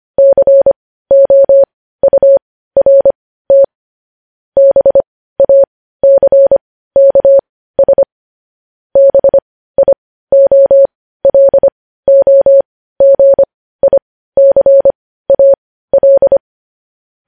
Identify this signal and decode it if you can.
Morse code, heard as a keyed tone, about 25 wpm, spaced as for 15 wpm, 550 Hz. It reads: COURT BACKS BIOLOGICAL
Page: news headlines in morse code at 15 wpm